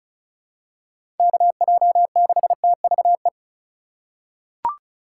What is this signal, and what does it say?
Morse code, heard as a keyed tone, about 34 wpm, 700 Hz. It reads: KJ6TVE
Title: USA Call Signs - 2x Character Spacing - 35wpm